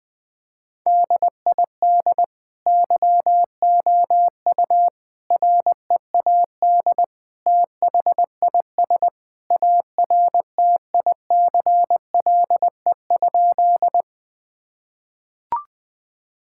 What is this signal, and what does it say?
Morse code, heard as a keyed tone, about 20 wpm, 700 Hz. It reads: DID YOU READ THIS ARTICLE?
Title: Sentences from Top 1000 Words - Review Part 5 - 20wpm